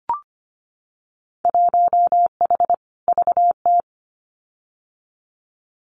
Morse code, heard as a keyed tone, about 25 wpm, 700 Hz. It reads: 1H VT
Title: Winter Field Day Exchanges Encore 25wpm